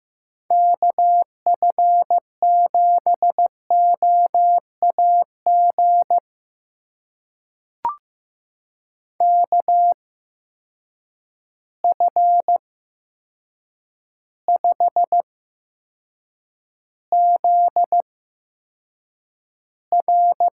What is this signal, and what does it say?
Morse code, heard as a keyed tone, about 15 wpm, spaced as for 7 wpm, 700 Hz. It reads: KF7OAG KF5ZR
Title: USA Call Signs - 6 Characters - 8x Character Spacing - 15wpm